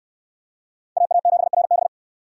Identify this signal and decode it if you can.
Morse code, heard as a keyed tone, about 69 wpm, 700 Hz. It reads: AA7RB